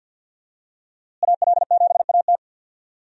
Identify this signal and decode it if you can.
Morse code, heard as a keyed tone, about 49 wpm, 700 Hz. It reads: AL7AT
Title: USA Call Signs - 5 Characters - Repeated 3x - 50wpm